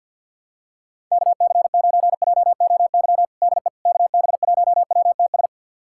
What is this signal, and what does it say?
Morse code, heard as a keyed tone, about 50 wpm, 690 Hz. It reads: KC9JOY DE KB1WTS
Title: USA Callsign DE Callsign - 6 Characters - 50wpm